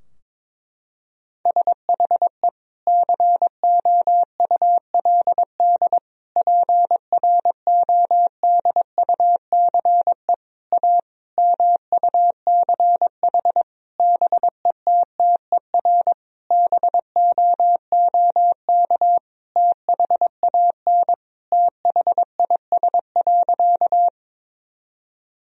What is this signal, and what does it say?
Morse code, heard as a keyed tone, about 22 wpm, 700 Hz. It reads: SHE COULD PRODUCE A MUCH BETTER BOOK THAN THIS.
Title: Sentences from Top 1000 Words - Review Part 3 - 22wpm